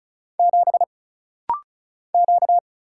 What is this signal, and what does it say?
Morse code, heard as a keyed tone, about 34 wpm, 700 Hz. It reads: 7 Q
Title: Single Letter-Number Alt - Rapid-Fire 35wpm